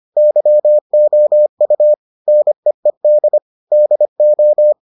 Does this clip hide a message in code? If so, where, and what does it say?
Yes: Morse code from 0.2 to 4.7 s, heard as a keyed tone, about 25 wpm, 600 Hz. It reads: YOU NEED DO